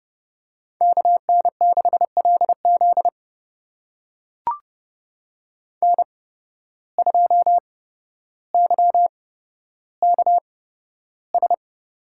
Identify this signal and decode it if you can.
Morse code, heard as a keyed tone, about 30 wpm, spaced as for 13 wpm, 700 Hz. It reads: KN6LZ N2YKS
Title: USA Call Signs - 5 Characters - 8x Character Spacing - 30wpm